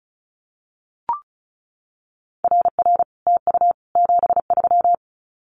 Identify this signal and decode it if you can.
Morse code, heard as a keyed tone, about 34 wpm, 700 Hz. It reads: RR TU 73
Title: Common QSO Phrases Encore 35wpm